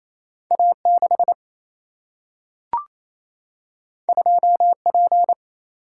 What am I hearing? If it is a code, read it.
Morse code, heard as a keyed tone, about 28 wpm, 700 Hz. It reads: A 6 2 P
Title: Single-Letter Single-Number 28wpm